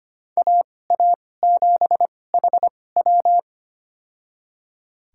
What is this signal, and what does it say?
Morse code, heard as a keyed tone, about 25 wpm, spaced as for 18 wpm, 700 Hz. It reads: AA7HW
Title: USA Call Signs - 5 Characters - 2x Character Spacing - 25wpm